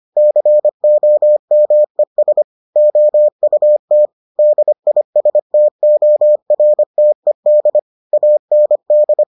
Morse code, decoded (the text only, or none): COMES OUT DISTORTED AND